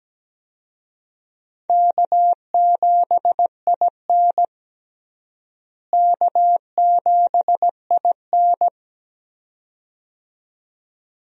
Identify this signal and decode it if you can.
Morse code, heard as a keyed tone, about 17 wpm, 700 Hz. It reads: K7IN K7IN
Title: USA Call Signs - 4 Characters - Repeated 2x - 17wpm